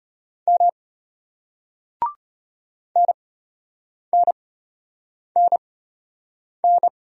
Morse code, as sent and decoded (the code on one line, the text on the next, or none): -- / -. -. -. -.
M NNNN